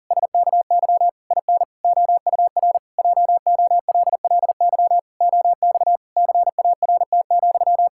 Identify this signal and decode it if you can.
Morse code, heard as a keyed tone, about 40 wpm, 700 Hz. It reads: SKY IN OUR JOLLY OX CART,
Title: Pangram Challenge v2 - 40wpm